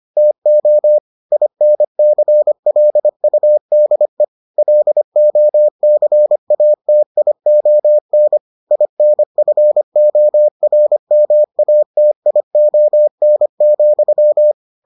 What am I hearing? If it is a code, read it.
Morse code, heard as a keyed tone, about 25 wpm, 600 Hz. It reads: TO INCLUDE LOCATION INFORMATION,